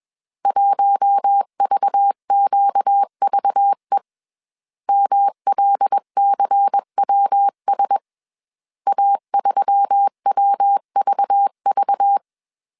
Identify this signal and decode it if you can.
Morse code, heard as a keyed tone, about 21 wpm, 800 Hz. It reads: 1VQVE MLCWS A3WVV
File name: CW / DET / 1 6 1500C S A 20wpm 2048